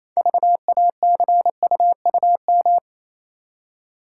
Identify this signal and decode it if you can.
Morse code, heard as a keyed tone, about 27 wpm, 700 Hz. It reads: VACUUM
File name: Easy Words - Speed-Racing with Hints - 22wpm